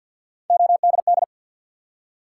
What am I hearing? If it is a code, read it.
Morse code, heard as a keyed tone, about 53 wpm, 700 Hz. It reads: ODD
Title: Three Letter Words - 50wpm